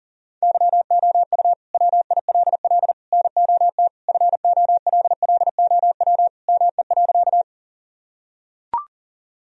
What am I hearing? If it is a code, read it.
Morse code, heard as a keyed tone, about 40 wpm, 700 Hz. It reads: YOU WILL NOT FOLLOW ME.
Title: Sentences from Top 1000 Words - Review Part 1 - 40wpm